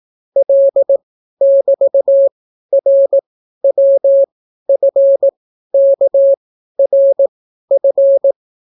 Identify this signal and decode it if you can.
Morse code, heard as a keyed tone, about 18 wpm, spaced as for 12 wpm, 550 Hz. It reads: L<BT>RWFKRF